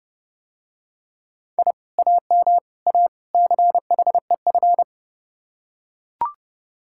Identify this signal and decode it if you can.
Morse code, heard as a keyed tone, about 32 wpm, 700 Hz. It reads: I AM A CHEF